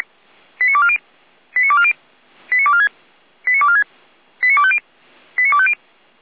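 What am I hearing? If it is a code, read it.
Five-tone selective call (ZVEI), heard as quick runs of five tones, tones of about 70 ms. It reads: 89240 89240 89246 89246 89250 89250
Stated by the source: ZVEI five-tone selective call 89160 received on SDR by feuerwehr-orlamuende (F85360) fire brigade.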